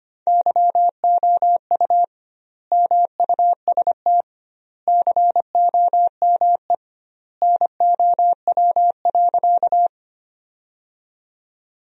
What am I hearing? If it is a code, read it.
Morse code, heard as a keyed tone, about 25 wpm, 700 Hz. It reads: YOU MUST COME NOW.
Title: 2x Word-Spacing - Sentences from Top 200 Words - 25wpm